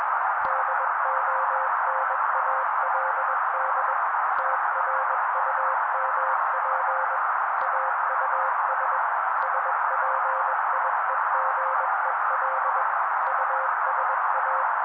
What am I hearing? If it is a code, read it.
Morse code, heard as a keyed tone, about 21 wpm, 530 Hz. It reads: DONALD TRUMP AUS SPIEGEL USA